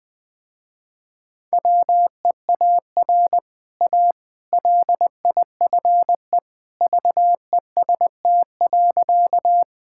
Morse code, heard as a keyed tone, about 20 wpm, 700 Hz. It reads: WEAR A LIFE VEST.
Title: Easy Sentences with Hints - 20wpm